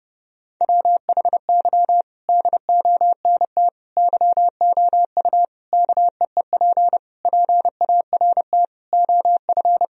Morse code, heard as a keyed tone, about 30 wpm, 700 Hz. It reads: WHY DONT YOU KEEP PART OF
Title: Sentences from Top 1000 Words - Review Part 4 - 30wpm